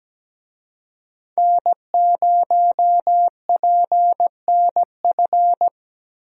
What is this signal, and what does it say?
Morse code, heard as a keyed tone, about 17 wpm, 700 Hz. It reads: N0PNF
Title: USA Call Signs - 5 Characters - 17wpm